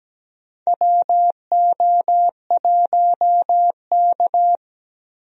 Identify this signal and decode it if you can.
Morse code, heard as a keyed tone, about 17 wpm, 700 Hz. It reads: WO1K
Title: USA Call Signs - 4 Characters - Repeated 2x - 17wpm